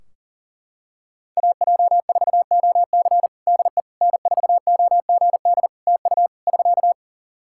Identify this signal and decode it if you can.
Morse code, heard as a keyed tone, about 37 wpm, 700 Hz. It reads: AJ4OC DE N4OGD TU <SK>
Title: Practice QSO - Exchange 4 - Ending - 40wpm